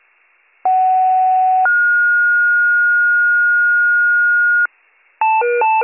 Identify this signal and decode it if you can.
Two-tone page, heard as a one-second tone, then a three-second tone, about 750 then 1430 Hz. It is heard from 0.6 to 4.7 s.